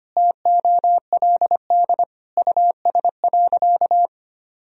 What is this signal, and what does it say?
Morse code, heard as a keyed tone, about 25 wpm, 700 Hz. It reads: TOLD US.
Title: Sentences from Top 1000 Words - Review Part 3 - 25wpm